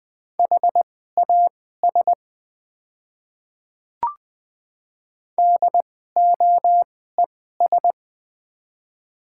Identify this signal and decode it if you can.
Morse code, heard as a keyed tone, about 21 wpm, spaced as for 15 wpm, 700 Hz. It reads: HAS DOES
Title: 2x Character Spacing - Top 500 Words - 20wpm